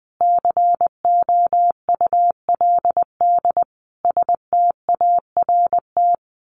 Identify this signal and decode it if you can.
Morse code, heard as a keyed tone, about 20 wpm, 700 Hz. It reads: COULD START
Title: A Sound Of Thunder 20wpm